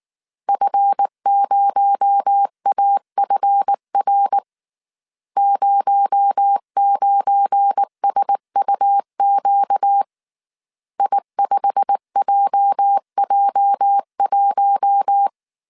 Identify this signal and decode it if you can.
Morse code, heard as a keyed tone, about 19 wpm, 800 Hz. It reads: F0AFR 09SUQ I5JJ1